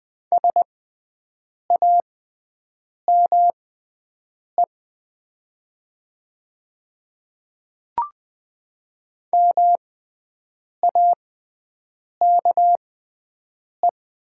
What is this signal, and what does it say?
Morse code, heard as a keyed tone, about 20 wpm, spaced as for 7 wpm, 700 Hz. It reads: SAME MAKE
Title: 6x Character Spacing - Top 200 Words - 20wpm